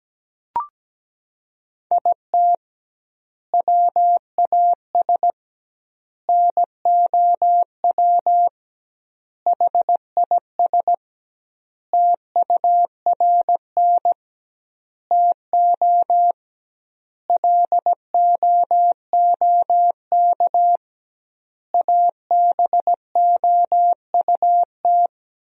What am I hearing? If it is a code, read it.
Morse code, heard as a keyed tone, about 17 wpm, 700 Hz. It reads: IT WAS NOW HIS TURN TO LOOK ABOUT
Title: 2x Word-Spacing - Sentences from Top 200 Words - 17wpm